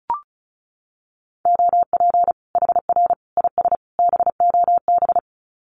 Morse code, heard as a keyed tone, about 34 wpm, 700 Hz. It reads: OP HR IS BOB
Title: Common QSO Phrases 35wpm